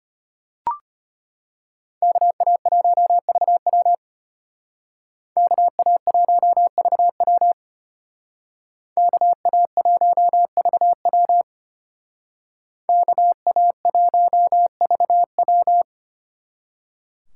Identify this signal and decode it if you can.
Morse code, heard as a keyed tone, about 32 wpm, 700 Hz. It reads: KA1VW KA1VW KA1VW KA1VW